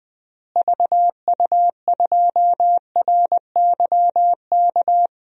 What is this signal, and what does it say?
Morse code, heard as a keyed tone, about 21 wpm, 700 Hz. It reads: VU2RYK
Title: DXCC-100 Call Signs - 6 Characters - Repeated 3x - 20wpm